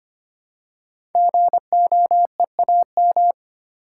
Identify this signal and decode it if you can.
Morse code, heard as a keyed tone, about 25 wpm, 700 Hz. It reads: GOEAM
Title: Code Groups - 5 Letters - 25wpm